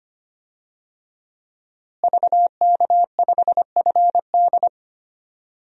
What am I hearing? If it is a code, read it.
Morse code, heard as a keyed tone, about 25 wpm, 700 Hz. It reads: VK5FD
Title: DXCC-100 Call Signs - 5 Characters - Repeated 3x - 25wpm